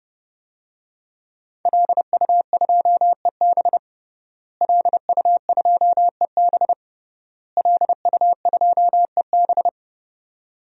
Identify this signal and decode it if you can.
Morse code, heard as a keyed tone, about 30 wpm, 700 Hz. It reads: LU2EB LU2EB LU2EB
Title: DXCC-100 Call Signs - 5 Characters - Repeated 3x - 30wpm